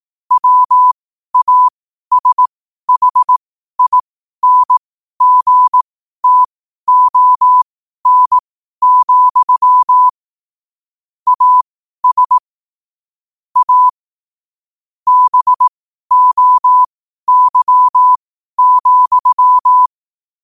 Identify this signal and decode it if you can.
Morse code, heard as a keyed tone, about 18 wpm, spaced as for 12 wpm, 1000 Hz. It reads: WASHINGTON, AS A BOY,
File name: qotdfiles / 12wpm